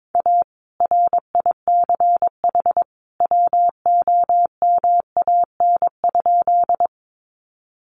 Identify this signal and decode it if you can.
Morse code, heard as a keyed tone, about 22 wpm, 700 Hz. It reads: A RICH WOMAN?
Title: Sentences from Top 1100 Words 22wpm